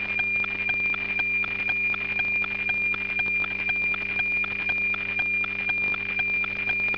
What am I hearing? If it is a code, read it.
Weather fax (radiofax).